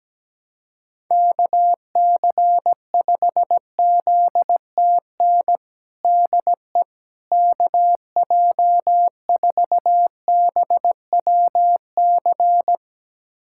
Morse code, heard as a keyed tone, about 17 wpm, 700 Hz. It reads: KC5ZTN DE KJ4BWC
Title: USA Callsign DE Callsign 17wpm